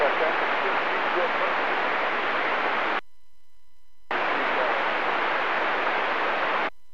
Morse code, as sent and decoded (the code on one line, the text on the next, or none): none